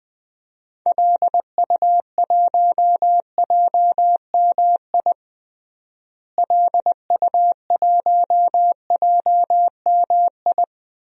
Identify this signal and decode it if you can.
Morse code, heard as a keyed tone, about 20 wpm, 700 Hz. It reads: LU1JMI LU1JMI
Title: DXCC-100 Call Signs - 6 Characters - Repeated 2x - 20wpm